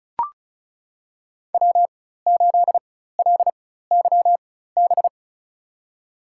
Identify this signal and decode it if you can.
Morse code, heard as a keyed tone, about 36 wpm, spaced as for 16 wpm, 700 Hz. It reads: W8LYB
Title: USA Call Signs - 5 Characters - 4x Character Spacing - 35wpm